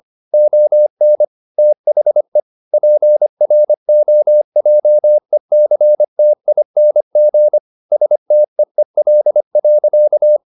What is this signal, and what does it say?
Morse code, heard as a keyed tone, about 25 wpm, 600 Hz. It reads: ON THE PROJECTING STEEL.
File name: the old man and the sea part 1 25wpm